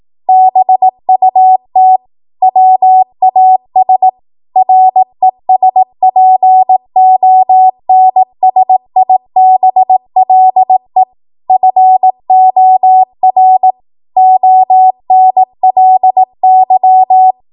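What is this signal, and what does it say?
Morse code, heard as a keyed tone, about 18 wpm, 750 Hz. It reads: BUT WAS RESPONSIBLE FOR ONLY
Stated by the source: Code practice at 18 WPM, a transition run bridging 15 to 20 WPM.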